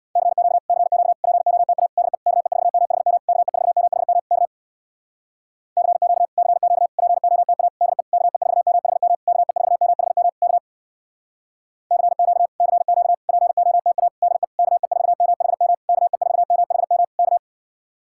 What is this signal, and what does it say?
Morse code, heard as a keyed tone, about 64 wpm, 700 Hz. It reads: CQ CQ POTA DE KE4MHM KE4MHM K CQ CQ POTA DE KE4MHM KE4MHM K CQ CQ POTA DE KE4MHM KE4MHM K